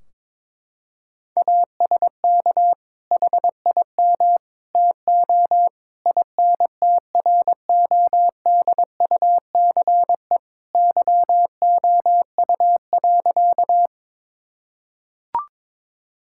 Morse code, heard as a keyed tone, about 22 wpm, 700 Hz. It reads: ASK HIM TO INTRODUCE YOU.
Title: Sentences from Top 1600 Words 22wpm